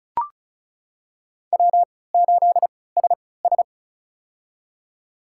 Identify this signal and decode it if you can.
Morse code, heard as a keyed tone, about 36 wpm, spaced as for 20 wpm, 700 Hz. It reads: W8SS
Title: USA Call Signs - 4 Characters - 3x Character Spacing - 35wpm